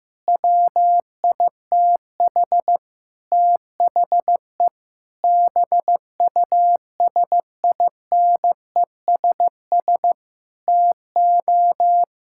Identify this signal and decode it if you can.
Morse code, heard as a keyed tone, about 15 wpm, 700 Hz. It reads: WITH THE BUSINESS TO
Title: Sentences from Top 1000 Words - Review Part 6 - 15wpm